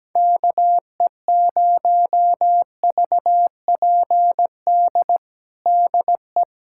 Morse code, heard as a keyed tone, about 17 wpm, 700 Hz. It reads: KE0VPD DE
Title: USA Callsign DE Callsign - 6 Characters - 17wpm